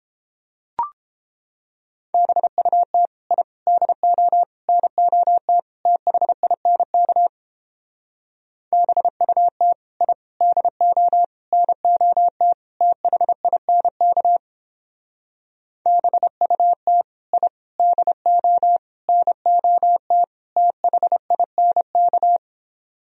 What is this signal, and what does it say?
Morse code, heard as a keyed tone, about 29 wpm, 700 Hz. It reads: BUT I DO NOT THINK BUT I DO NOT THINK BUT I DO NOT THINK